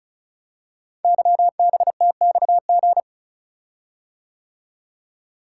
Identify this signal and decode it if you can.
Morse code, heard as a keyed tone, about 36 wpm, 700 Hz. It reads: YBTXG
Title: Code Groups - 5 Letters - 35wpm